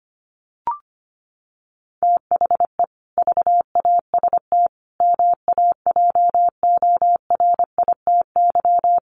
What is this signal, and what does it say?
Morse code, heard as a keyed tone, about 25 wpm, 700 Hz. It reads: THE VAST MAJORITY
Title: Sets of 5 Words Encore 25wpm